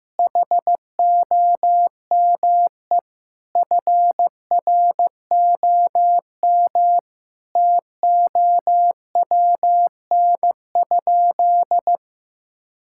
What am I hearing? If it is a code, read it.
Morse code, heard as a keyed tone, about 15 wpm, 700 Hz. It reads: HOME FROM TOWN?
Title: Sentences from Top 1000 Words - Review Part 5 - 15wpm